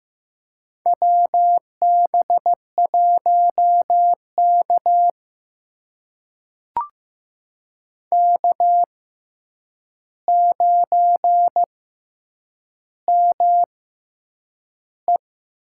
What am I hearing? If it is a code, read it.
Morse code, heard as a keyed tone, about 15 wpm, spaced as for 8 wpm, 700 Hz. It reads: WB1K K9ME
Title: USA Call Signs - 6x Character Spacing - 15wpm